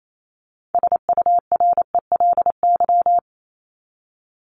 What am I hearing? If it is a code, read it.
Morse code, heard as a keyed tone, about 28 wpm, 700 Hz. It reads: SURELY
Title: LY Suffix - Top 100 Words with Base Words 28wpm